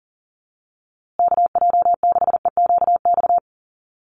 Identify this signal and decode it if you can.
Morse code, heard as a keyed tone, about 40 wpm, 700 Hz. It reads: KJ6EQX
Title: USA Call Signs Repeated 2x 40wpm